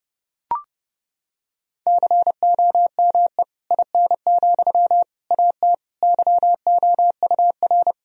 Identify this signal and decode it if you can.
Morse code, heard as a keyed tone, about 30 wpm, 700 Hz. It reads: COME IN, AT YOUR
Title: Sentences from Top 1000 Words - Review Part 4 - 30wpm